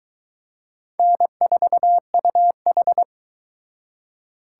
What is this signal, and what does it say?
Morse code, heard as a keyed tone, about 23 wpm, 700 Hz. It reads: N4UH